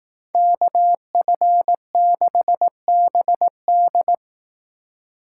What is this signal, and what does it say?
Morse code, heard as a keyed tone, about 18 wpm, 700 Hz. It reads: KF6BD
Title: Speed Racing - USA 2x2 Call Signs 15wpm